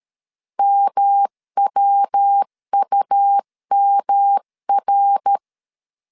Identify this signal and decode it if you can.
Morse code, heard as a keyed tone, about 13 wpm, 800 Hz. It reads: MWUMR
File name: CW / DET / 1 3 1500C S A 12wpm 2905